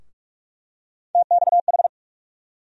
Morse code, heard as a keyed tone, about 46 wpm, 700 Hz. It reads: TXH
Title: USA Call Sign Postfixes - 3 Letters 45wpm